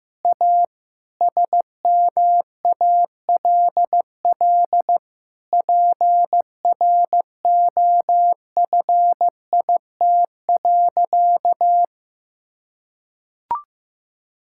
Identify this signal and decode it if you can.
Morse code, heard as a keyed tone, about 15 wpm, 700 Hz. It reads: A SMALL PROFIT.
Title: Sentences from Top 1000 Words - Review Part 5 - 15wpm